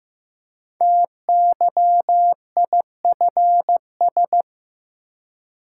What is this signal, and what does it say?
Morse code, heard as a keyed tone, about 15 wpm, 700 Hz. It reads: TYIFS
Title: Code Groups - 5 Letter:Numbers - 15wpm